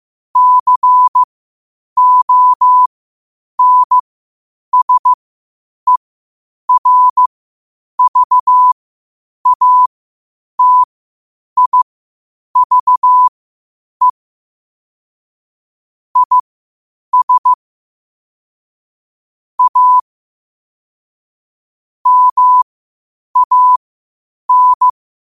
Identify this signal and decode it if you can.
Morse code, heard as a keyed tone, about 15 wpm, spaced as for 8 wpm, 1000 Hz. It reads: CONSERVATIVE IS A MAN
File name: qotdfiles / 8wpm